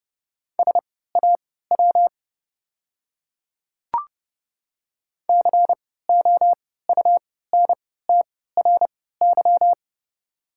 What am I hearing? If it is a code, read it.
Morse code, heard as a keyed tone, about 30 wpm, spaced as for 17 wpm, 700 Hz. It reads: SAW COUNTRY
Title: 3x Character Spacing - Top 300 Words - 30wpm